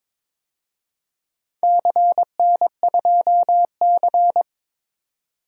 Morse code, heard as a keyed tone, about 22 wpm, 700 Hz. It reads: CN2C
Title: DXCC-100 Call Signs - 4 Characters - 22wpm